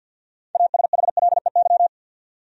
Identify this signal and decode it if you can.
Morse code, heard as a keyed tone, about 53 wpm, 700 Hz. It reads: ASHLEY
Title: USA Common Names - 50wpm